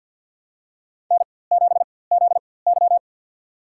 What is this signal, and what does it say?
Morse code, heard as a keyed tone, about 52 wpm, spaced as for 23 wpm, 700 Hz. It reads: N7ZY
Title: USA Call Signs - 4x Character Spacing - 50wpm